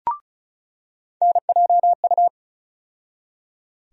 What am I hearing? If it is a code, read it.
Morse code, heard as a keyed tone, about 36 wpm, 700 Hz. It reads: NJU
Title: USA Call Sign Postfixes - 3 Letters 35wpm